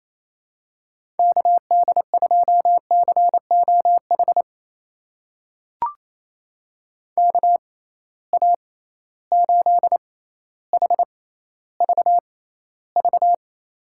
Morse code, heard as a keyed tone, about 28 wpm, spaced as for 14 wpm, 700 Hz. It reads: KD2COH KA8HVV
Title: USA Call Signs - 6 Characters - 6x Character Spacing - 28wpm